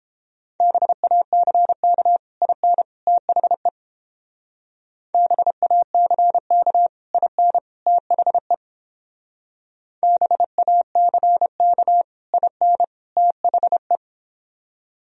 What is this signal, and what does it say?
Morse code, heard as a keyed tone, about 31 wpm, 700 Hz. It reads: BACK IN THE BACK IN THE BACK IN THE